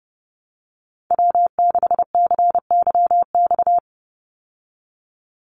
Morse code, heard as a keyed tone, about 30 wpm, 700 Hz. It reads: W6CYX
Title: Call Signs Repeated 3x - 30wpm